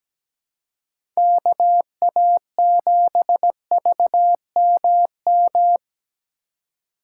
Morse code, heard as a keyed tone, about 17 wpm, 700 Hz. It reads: KA7VMM
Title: USA Call Signs - 3x Character Spacing - 17wpm